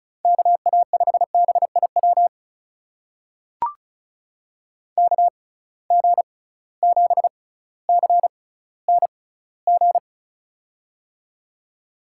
Morse code, heard as a keyed tone, about 35 wpm, spaced as for 18 wpm, 700 Hz. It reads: KA5BIW KG7CNG
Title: USA Call Signs - 6 Characters - 6x Character Spacing - 35wpm